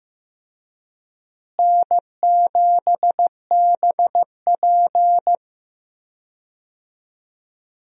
Morse code, heard as a keyed tone, about 15 wpm, 700 Hz. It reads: N7BP
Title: USA Call Signs - 4 Characters - Repeated 2x - 15wpm